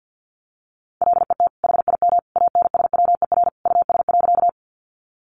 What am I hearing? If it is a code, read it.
Morse code, heard as a keyed tone, about 50 wpm, 700 Hz. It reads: LET HIM ANSWER US.
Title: Sentences from Top 300 Words 50wpm